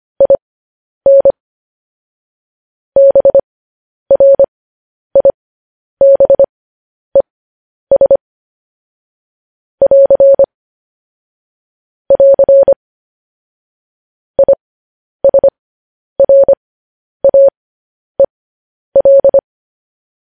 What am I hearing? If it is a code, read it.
Morse code, heard as a keyed tone, about 25 wpm, spaced as for 10 wpm, 550 Hz. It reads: IN BRIBES <AR> <AR> ISRAEL
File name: podcast / News headlines-1x-25x10